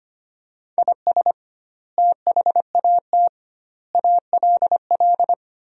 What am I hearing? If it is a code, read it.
Morse code, heard as a keyed tone, about 25 wpm, 700 Hz. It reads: IS THAT ALL
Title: 2x Word-Spacing - Sentences from Top 200 Words - 25wpm